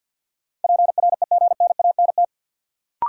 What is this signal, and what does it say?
Morse code, heard as a keyed tone, about 52 wpm, 700 Hz. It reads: PREGNANT
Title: Sentences from Top 1000 Words - Review Part 6 - New Word Drill - 50wpm